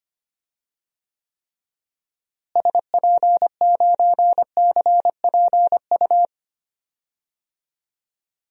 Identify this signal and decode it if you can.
Morse code, heard as a keyed tone, about 25 wpm, 700 Hz. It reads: SP9CPU
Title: DXCC-100 Call Signs - 6 Characters - Repeated 3x - 25wpm